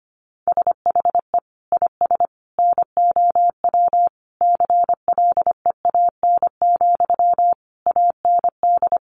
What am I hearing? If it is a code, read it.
Morse code, heard as a keyed tone, about 25 wpm, 700 Hz. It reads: SHE IS NOW CLEAN, AND
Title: Sentences from Top 1100 Words 25wpm